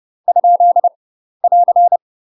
Morse code, heard as a keyed tone, about 30 wpm, 700 Hz. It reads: ? <AR>
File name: UK MASTER QSOs TRAINING / 30WPM / UK MASTER QSOs 0008